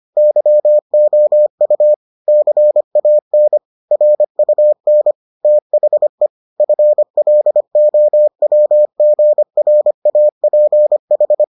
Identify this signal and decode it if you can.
Morse code, heard as a keyed tone, about 25 wpm, 600 Hz. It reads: YOU CAN RUN THE FLOWGRAPH